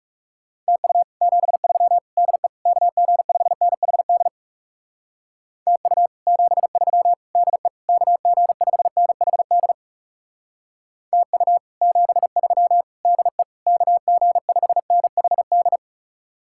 Morse code, heard as a keyed tone, about 39 wpm, 700 Hz. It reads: TU 73 DE KG5NHD TU 73 DE KG5NHD TU 73 DE KG5NHD